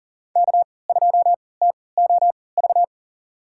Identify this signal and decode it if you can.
Morse code, heard as a keyed tone, about 40 wpm, spaced as for 23 wpm, 700 Hz. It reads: K2TOV
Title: USA Call Signs - 5 Characters - 3x Character Spacing - 40wpm